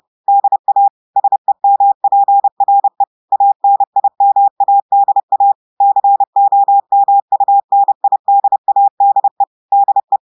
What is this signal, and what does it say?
Morse code, heard as a keyed tone, about 30 wpm, 800 Hz. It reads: DA SEMPRE ANIMADA COMUNIDADE DE
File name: audios / portugais / portugais texte en clair 1 vitesse 30 mots minutes